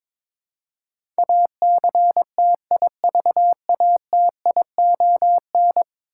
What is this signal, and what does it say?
Morse code, heard as a keyed tone, about 22 wpm, 700 Hz. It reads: ACTIVATION
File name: practice / 22wpm / 10 Letter Words - First 3 Letters Given - 22wpm